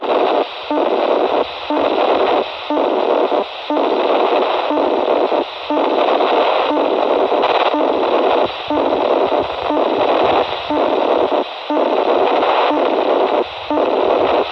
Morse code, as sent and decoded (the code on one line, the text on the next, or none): none